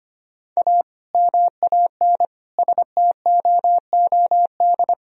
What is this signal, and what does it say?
Morse code, heard as a keyed tone, about 25 wpm, 700 Hz. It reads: A MAN STOOD
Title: Sentences from Top 1000 Words - Review Part 2 - 25wpm